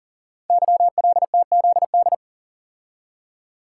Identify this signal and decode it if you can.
Morse code, heard as a keyed tone, about 42 wpm, 690 Hz. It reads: YLTZD
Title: Code Groups - 5 Letter:Numbers - 40wpm